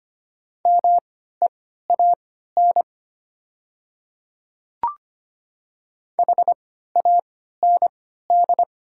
Morse code, heard as a keyed tone, about 25 wpm, spaced as for 14 wpm, 700 Hz. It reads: MEAN HAND
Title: 3x Character Spacing - Top 200 Words - 25wpm